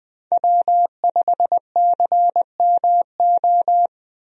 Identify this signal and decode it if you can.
Morse code, heard as a keyed tone, about 20 wpm, 700 Hz. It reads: W5CMO
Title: USA Call Signs - 5 Characters - Repeated 2x - 20wpm